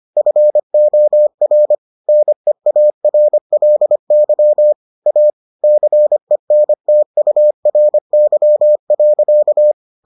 Morse code, heard as a keyed tone, about 25 wpm, 600 Hz. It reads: FOR NEARLY A CENTURY.